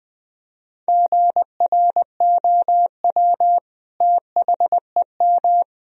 Morse code, heard as a keyed tone, about 20 wpm, 700 Hz. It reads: GROW THEM
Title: Sentences from Top 1000 Words - Review Part 1 - 20wpm